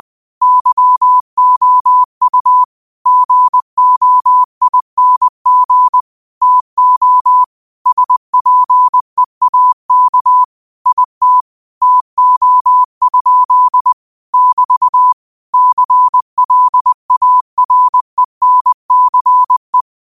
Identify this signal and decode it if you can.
Morse code, heard as a keyed tone, about 20 wpm, 1000 Hz. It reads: YOU GOING TO SPEAK IT TO? <BT> CLARENCE
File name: qotdfiles / 20wpm